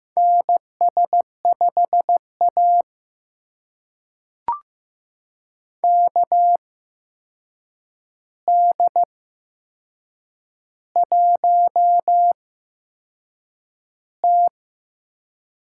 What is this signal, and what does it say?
Morse code, heard as a keyed tone, about 15 wpm, spaced as for 6 wpm, 700 Hz. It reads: NS5A KD1T
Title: USA Call Signs - 4 Characters - 8x Character Spacing - 15wpm